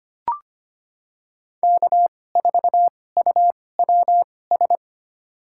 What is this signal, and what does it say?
Morse code, heard as a keyed tone, about 25 wpm, spaced as for 18 wpm, 700 Hz. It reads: K4UWS
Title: USA Call Signs - 5 Characters - 2x Character Spacing - 25wpm